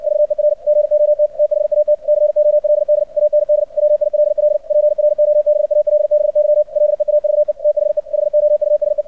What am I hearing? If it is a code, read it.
Morse code, heard as a keyed tone, about 56 wpm, 600 Hz. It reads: VER GOT THAT COLD AND WILL PROBALBY LEAVE THE HOUSE
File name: archived / CRSnet